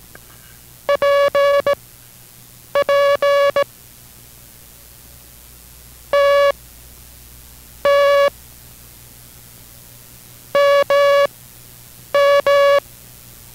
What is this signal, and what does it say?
Morse code, heard as a keyed tone, about 18 wpm, spaced as for 7 wpm, 570 Hz. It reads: PP TT MM